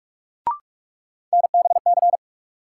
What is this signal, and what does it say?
Morse code, heard as a keyed tone, about 43 wpm, 700 Hz. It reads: NBC
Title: Sets of 5 Words Encore - New Word Drill - 45wpm